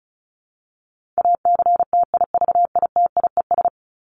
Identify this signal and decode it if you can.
Morse code, heard as a keyed tone, about 34 wpm, 700 Hz. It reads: ACTIVITIES
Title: ES:S Suffix - Top 100 Words with Base Words 35wpm